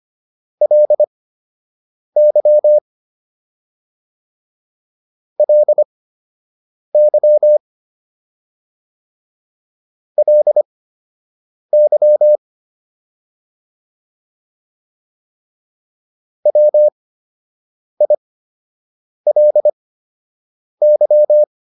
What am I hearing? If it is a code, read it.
Morse code, heard as a keyed tone, about 25 wpm, spaced as for 7 wpm, 600 Hz. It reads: LY LY LY WILY
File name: LY4 07